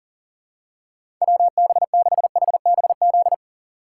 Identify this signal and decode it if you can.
Morse code, heard as a keyed tone, about 40 wpm, 700 Hz. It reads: WB6HBZ
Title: USA Call Signs - 6 Characters - 6x Character Spacing - 40wpm